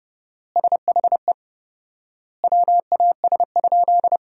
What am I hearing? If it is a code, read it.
Morse code, heard as a keyed tone, about 30 wpm, 700 Hz. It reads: SHE WAS?
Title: 4x Word-Spacing - Sentences from Top 200 Words - 30wpm